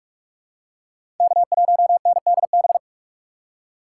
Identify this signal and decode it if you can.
Morse code, heard as a keyed tone, about 45 wpm, 700 Hz. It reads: K1NDB
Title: USA Call Signs - 3x Character Spacing - 45wpm